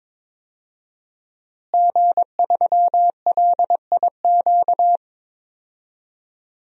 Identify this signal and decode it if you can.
Morse code, heard as a keyed tone, about 22 wpm, 700 Hz. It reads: G3LIQ
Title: DXCC-100 Call Signs - 5 Characters - Repeated 3x - 22wpm